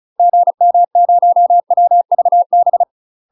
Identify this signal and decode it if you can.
Morse code, heard as a keyed tone, about 35 wpm, 700 Hz. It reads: GM0WVB
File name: UK MASTER QSOs TRAINING/35WPM